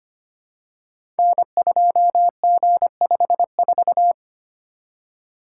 Morse code, heard as a keyed tone, about 25 wpm, 700 Hz. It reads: N2G54
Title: Code Groups - 5 Letter:Numbers - 25wpm